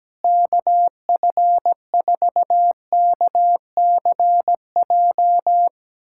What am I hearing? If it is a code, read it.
Morse code, heard as a keyed tone, about 17 wpm, 700 Hz. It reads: KF4KCJ
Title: USA Callsign DE Callsign - 6 Characters - 17wpm